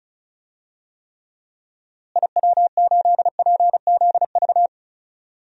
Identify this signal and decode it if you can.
Morse code, heard as a keyed tone, about 36 wpm, 700 Hz. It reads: IW8PZV
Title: DXCC-100 Call Signs - 6 Characters - Repeated 3x - 35wpm